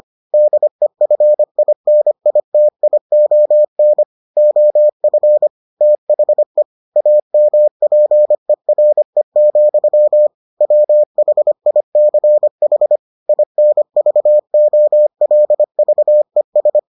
Morse code, heard as a keyed tone, about 25 wpm, 600 Hz. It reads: DEFINITION OF THE AMPERE, WHICH INVOLVES